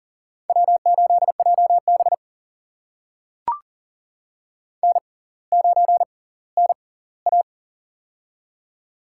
Morse code, heard as a keyed tone, about 40 wpm, spaced as for 20 wpm, 700 Hz. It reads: W8JB N9NA
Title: USA Call Signs - 4 Characters - 6x Character Spacing - 40wpm